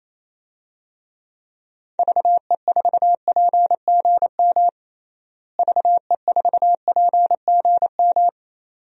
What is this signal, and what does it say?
Morse code, heard as a keyed tone, about 28 wpm, 700 Hz. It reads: VE4PGM VE4PGM
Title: DXCC-100 Call Signs - 6 Characters - Repeated 2x - 28wpm